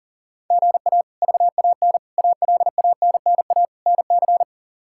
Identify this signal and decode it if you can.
Morse code, heard as a keyed tone, about 40 wpm, 700 Hz. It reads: GA VAN ALANNA NC
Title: K1USN SST QSO Practice - 40wpm